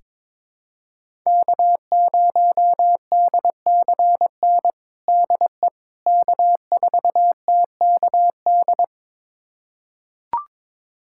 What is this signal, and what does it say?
Morse code, heard as a keyed tone, about 22 wpm, 700 Hz. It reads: K0DCN DE K4TKD
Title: USA Callsign DE Callsign - 5 Characters - 22wpm